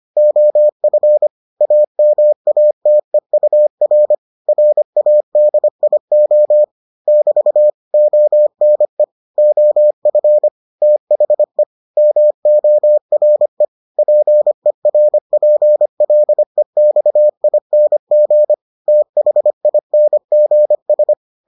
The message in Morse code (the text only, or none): OF AMATEUR RADIO <BT> ONE OF THE MORE PERPLEXING THINGS